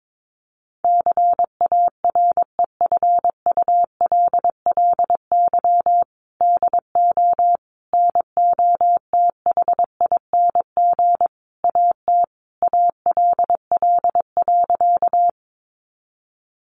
Morse code, heard as a keyed tone, about 22 wpm, 700 Hz. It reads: CAREFULLY DO NOTHING AT ALL.
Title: Sentences from Top 1300 Words 22wpm